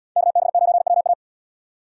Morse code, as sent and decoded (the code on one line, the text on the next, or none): -.- -... ----. .-- .-
KB9WA